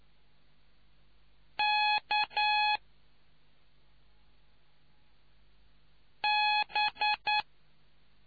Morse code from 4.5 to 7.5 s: B